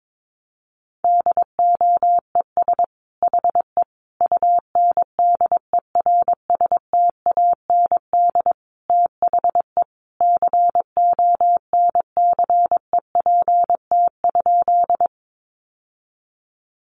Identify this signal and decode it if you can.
Morse code, heard as a keyed tone, about 22 wpm, 700 Hz. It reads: DOES HE UNDERSTAND THE CONCEPT?
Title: Sentences from Top 1100 Words 22wpm